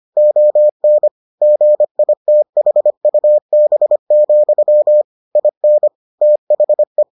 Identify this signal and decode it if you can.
Morse code, heard as a keyed tone, about 25 wpm, 600 Hz. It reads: ON GITHUB, IN THE